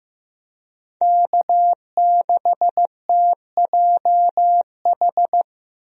Morse code, heard as a keyed tone, about 15 wpm, 700 Hz. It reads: K6TJH
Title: USA Call Signs - 5 Characters - 15wpm